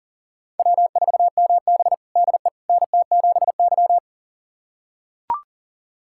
Morse code, heard as a keyed tone, about 40 wpm, 700 Hz. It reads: W4MB DE NT7Y
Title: USA Callsign DE Callsign - 4 Characters - 40wpm